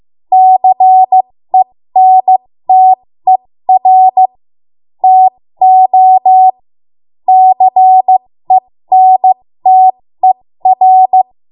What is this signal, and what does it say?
Morse code, heard as a keyed tone, about 15 wpm, spaced as for 13 wpm, 750 Hz. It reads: CENTER TO CENTER
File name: Morse / 13WPM